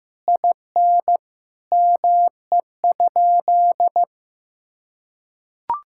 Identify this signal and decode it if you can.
Morse code, heard as a keyed tone, about 15 wpm, 700 Hz. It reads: IN ME?
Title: Sentences from Top 1000 Words - Review Part 6 - 15wpm